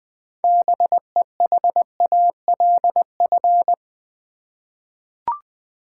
Morse code, heard as a keyed tone, about 20 wpm, 700 Hz. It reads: BEHALF
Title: 6 Letter Words - First 3 Letters Given - 20wpm